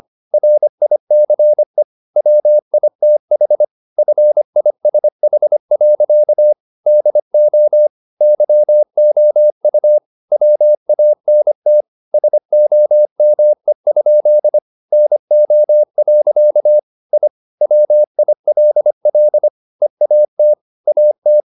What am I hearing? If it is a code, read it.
Morse code, heard as a keyed tone, about 25 wpm, 600 Hz. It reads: RICE WITH FISH. DO YOU WANT SOME? NO. I WILL EAT AT
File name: the old man and the sea part 1 25wpm